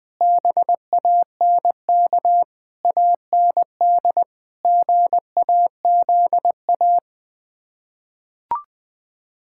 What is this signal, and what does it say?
Morse code, heard as a keyed tone, about 21 wpm, 700 Hz. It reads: BANK AND GAZA
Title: Build A Phrase - 5-Word Phrases - 20wpm